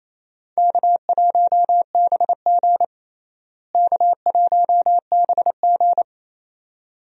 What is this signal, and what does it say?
Morse code, heard as a keyed tone, about 28 wpm, 700 Hz. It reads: K1BG K1BG
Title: USA Call Signs - 4 Characters - Repeated 2x - 28wpm